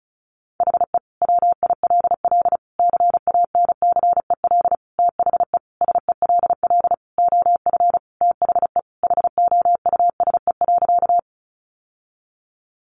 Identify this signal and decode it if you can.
Morse code, heard as a keyed tone, about 34 wpm, 700 Hz. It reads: HE WILL CANCEL THE SELL OF THE HOUSE.
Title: Sentences from Top 1200 Words 35wpm